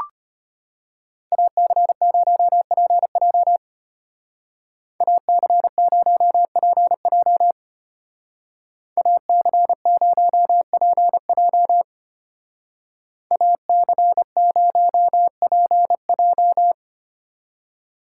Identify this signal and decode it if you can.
Morse code, heard as a keyed tone, about 32 wpm, 700 Hz. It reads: AC0PJ AC0PJ AC0PJ AC0PJ